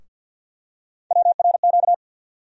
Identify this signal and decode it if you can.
Morse code, heard as a keyed tone, about 53 wpm, 700 Hz. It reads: WRQ
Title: USA Call Sign Postfixes - 3 Letters 50wpm